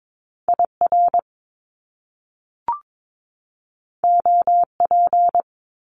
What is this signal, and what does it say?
Morse code, heard as a keyed tone, about 22 wpm, 700 Hz. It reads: IR OP
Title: Top Two Letter Combinations in Words 22wpm